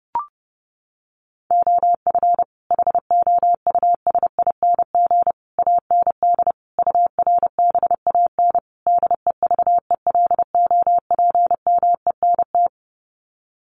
Morse code, heard as a keyed tone, about 30 wpm, 700 Hz. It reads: OF HOUSING AND URBAN DEVELOPMENT
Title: Sets of 5 Words Encore2 30wpm